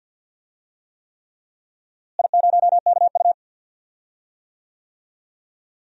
Morse code, heard as a keyed tone, about 52 wpm, 700 Hz. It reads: I0KU
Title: DXCC-100 Call Signs - 4 Characters - Repeated 3x - 50wpm